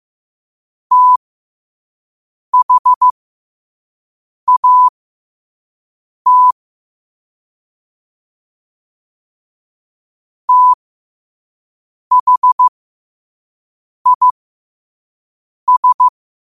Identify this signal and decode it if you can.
Morse code, heard as a keyed tone, about 15 wpm, spaced as for 5 wpm, 1000 Hz. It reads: THAT THIS
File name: qotdfiles / 5wpm